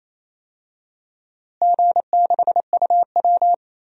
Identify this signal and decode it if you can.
Morse code, heard as a keyed tone, about 28 wpm, 700 Hz. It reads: G6UW
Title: DXCC-100 Call Signs - 4 Characters - Repeated 2x - 28wpm